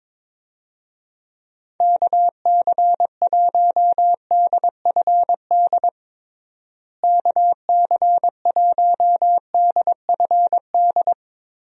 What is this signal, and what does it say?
Morse code, heard as a keyed tone, about 22 wpm, 700 Hz. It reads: KC1DFD KC1DFD
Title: USA Call Signs - 6 Characters - Repeated 2x - 22wpm